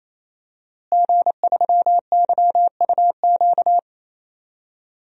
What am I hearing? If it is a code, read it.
Morse code, heard as a keyed tone, about 28 wpm, 700 Hz. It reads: G3YUQ
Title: DXCC-100 Call Signs - 5 Characters - Repeated 3x - 28wpm